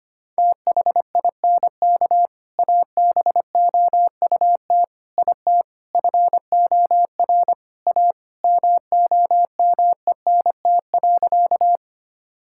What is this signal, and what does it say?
Morse code, heard as a keyed tone, about 25 wpm, 700 Hz. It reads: THINK ABOUT IT FOR A MOMENT.
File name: practice / 25wpm / Sentences with 3-Word Phrases - 25wpm